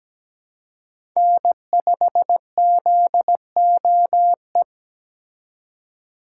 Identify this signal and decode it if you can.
Morse code, heard as a keyed tone, about 21 wpm, 700 Hz. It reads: N5ZOE